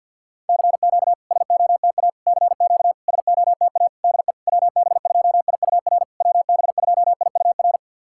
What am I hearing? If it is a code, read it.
Morse code, heard as a keyed tone, about 50 wpm, 700 Hz. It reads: CQ SOTA CQ SOTA DE WB2IUR WB2IUR
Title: SOTA QSO Practice 50wpm